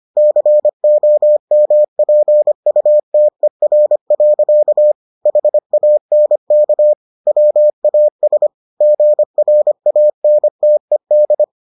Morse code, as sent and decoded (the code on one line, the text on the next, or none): -.-. --- -- .--. ..- - . .-. .-.-.- / .... .- -. -.- / .-- .- ... / --. .-. .- -. - . -..
COMPUTER. HANK WAS GRANTED